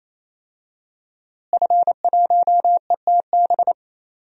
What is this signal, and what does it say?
Morse code, heard as a keyed tone, about 27 wpm, 700 Hz. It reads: F1ETB
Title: DXCC-100 Call Signs - 5 Characters - Repeated 3x - 28wpm